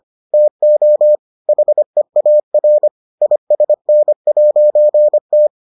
Morse code, heard as a keyed tone, about 25 wpm, 600 Hz. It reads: TO HEAR ISN'T